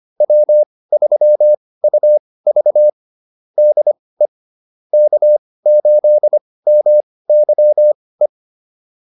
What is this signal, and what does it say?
Morse code, heard as a keyed tone, about 25 wpm, spaced as for 18 wpm, 600 Hz. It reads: W3UV DE K8MYE
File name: QSO403 18